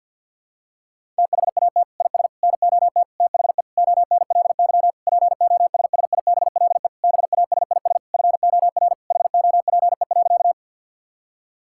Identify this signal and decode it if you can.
Morse code, heard as a keyed tone, about 51 wpm, 700 Hz. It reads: THAT IS NOT THE ONLY POSSIBLE BASIS FOR HOPE.
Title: Sentences from Top 1000 Words - Review Part 5 - 50wpm